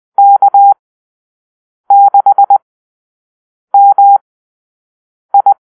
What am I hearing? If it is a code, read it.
Morse code, heard as a keyed tone, about 20 wpm, spaced as for 6 wpm, 800 Hz. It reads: K6MI
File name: arrl / Call-Sign Drill-6